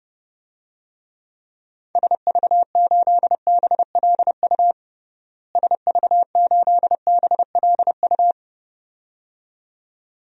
Morse code, heard as a keyed tone, about 29 wpm, 700 Hz. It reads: SV8BLU SV8BLU
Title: DXCC-100 Call Signs - 6 Characters - Repeated 2x - 30wpm